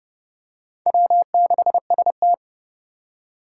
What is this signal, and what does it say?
Morse code, heard as a keyed tone, about 30 wpm, 700 Hz. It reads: W6ST
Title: USA Call Signs - 4 Characters - 4x Character Spacing - 30wpm